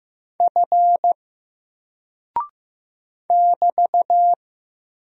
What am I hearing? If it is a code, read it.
Morse code, heard as a keyed tone, about 15 wpm, 700 Hz. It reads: F <BT>
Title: Single Character (Letter:Number:Punctuation:Prosign) Alt 15wpm